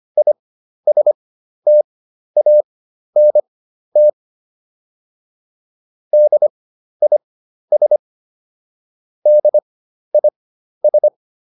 Morse code, heard as a keyed tone, about 25 wpm, spaced as for 12 wpm, 600 Hz. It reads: ISTANT DIS DIS